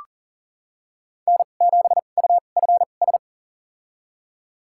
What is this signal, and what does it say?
Morse code, heard as a keyed tone, about 40 wpm, spaced as for 29 wpm, 700 Hz. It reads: N7UFS
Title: USA Call Signs - 5 Characters - 2x Character Spacing - 40wpm